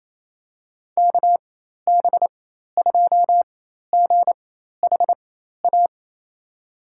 Morse code, heard as a keyed tone, about 28 wpm, spaced as for 13 wpm, 700 Hz. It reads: KB2GHA
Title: USA Call Signs - 4x Character Spacing - 28wpm